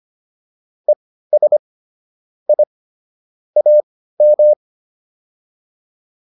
Morse code, heard as a keyed tone, about 25 wpm, spaced as for 15 wpm, 600 Hz. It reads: ES I AM